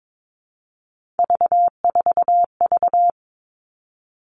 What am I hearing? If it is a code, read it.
Morse code, heard as a keyed tone, about 22 wpm, 700 Hz. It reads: V4V
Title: All Cut Numbers - 3-Digit 22wpm